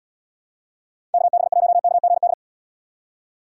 Morse code, heard as a keyed tone, about 78 wpm, 700 Hz. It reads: KB1KGD